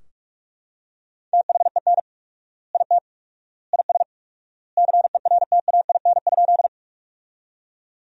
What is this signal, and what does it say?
Morse code, heard as a keyed tone, about 44 wpm, 700 Hz. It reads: THEN IT IS CERTAIN?
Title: 4x Word-Spacing - Sentences from Top 500 Words - 45wpm